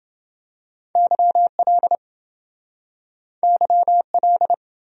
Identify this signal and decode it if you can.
Morse code, heard as a keyed tone, about 28 wpm, 700 Hz. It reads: YL YL